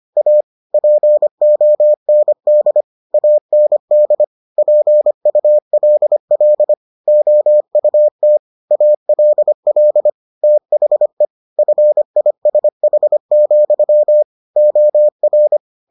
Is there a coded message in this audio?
Yes: Morse code from 0.2 to 15.6 s, heard as a keyed tone, about 25 wpm, 600 Hz. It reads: A POND AND PULL OUT ALL THE FISH, OR